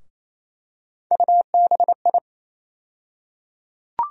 Morse code, heard as a keyed tone, about 28 wpm, 700 Hz. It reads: UBI
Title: USA Call Sign Postfixes - 3 Letters 28wpm